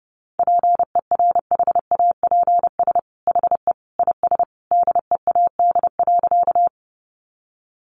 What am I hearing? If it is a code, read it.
Morse code, heard as a keyed tone, about 30 wpm, 700 Hz. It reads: PERHAPS HE IS DEAD.
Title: Sentences from Top 1500 Words 30wpm